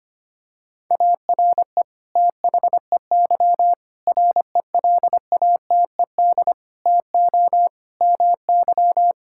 Morse code, heard as a keyed tone, about 25 wpm, 700 Hz. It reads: ARE THEY RELATED TO MY
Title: Sentences from Top 1000 Words - Review Part 5 - 25wpm